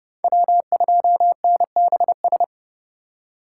Morse code, heard as a keyed tone, about 30 wpm, 700 Hz. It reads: W2NBS
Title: USA Call Signs - 4x Character Spacing - 30wpm